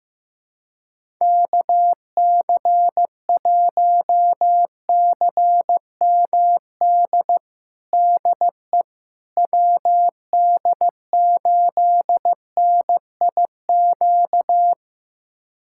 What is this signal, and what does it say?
Morse code, heard as a keyed tone, about 15 wpm, 700 Hz. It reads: KC1CMD DE WD8NIQ
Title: USA Callsign DE Callsign 15wpm